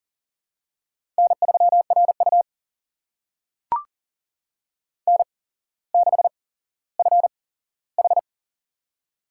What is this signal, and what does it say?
Morse code, heard as a keyed tone, about 39 wpm, spaced as for 17 wpm, 700 Hz. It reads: N3RU N6FH
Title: USA Call Signs - 4 Characters - 8x Character Spacing - 40wpm